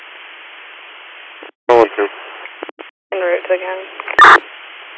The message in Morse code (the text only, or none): none